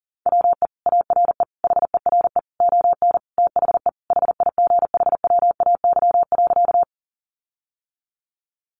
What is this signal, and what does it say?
Morse code, heard as a keyed tone, about 40 wpm, 700 Hz. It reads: WE ARE HERE ON THE HIGHWAY.
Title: Sentences from Top 1400 Words 40wpm